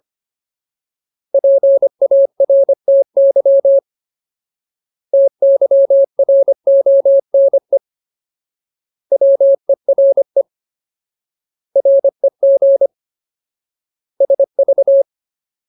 Morse code, heard as a keyed tone, about 25 wpm, 550 Hz. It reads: PARTY TYRONE WERE REG SV
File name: Ultimate Word List / 25WPM Ultimate